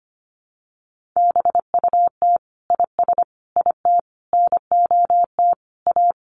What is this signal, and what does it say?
Morse code, heard as a keyed tone, about 25 wpm, 700 Hz. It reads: BUT IS IT NOT A
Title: Sentences from Top 1500 Words 25wpm